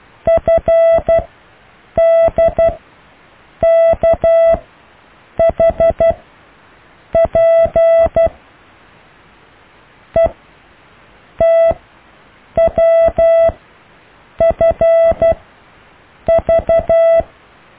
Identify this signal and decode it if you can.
Morse code, heard as a keyed tone, about 12 wpm, spaced as for 7 wpm, 660 Hz. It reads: FDKHP ETWFV
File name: cw / EKTPV+BFHNW